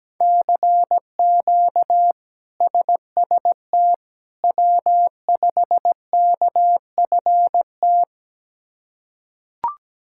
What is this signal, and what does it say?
Morse code, heard as a keyed tone, about 17 wpm, 700 Hz. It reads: CQ SST W5KFT
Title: K1USN SST QSO Practice - 17wpm